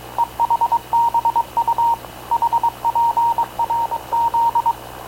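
Morse code, heard as a keyed tone, about 22 wpm, 930 Hz. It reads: EHBU HPRZ
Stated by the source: Havana Radio / CLA sending Spanish press (PX) in Morse, recorded by Radio Officer Paul Dumesnil onboard CNAV QUEST/CZDO.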